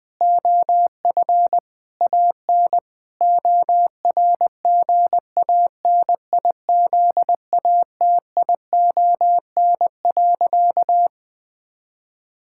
Morse code, heard as a keyed tone, about 21 wpm, 700 Hz. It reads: OF AN ORGANIZATION.
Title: Sentences from Top 1000 Words - Review Part 5 - 20wpm